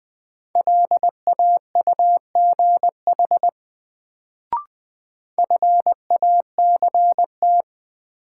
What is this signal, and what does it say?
Morse code, heard as a keyed tone, about 21 wpm, 700 Hz. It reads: LAUGH FACT
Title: Sentences from Top 500 Words - New Word Drill - 20wpm